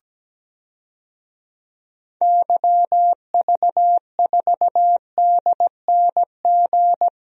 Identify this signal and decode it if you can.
Morse code, heard as a keyed tone, about 17 wpm, 700 Hz. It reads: YV4DNG
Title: DXCC-100 Call Signs - 6 Characters - 17wpm